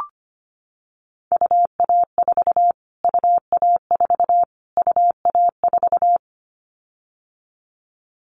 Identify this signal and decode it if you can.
Morse code, heard as a keyed tone, about 25 wpm, 700 Hz. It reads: UA4 UA4 UA4
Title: Callsign Prefixes for Top 25 DXCC - Repeated 3x - 25wpm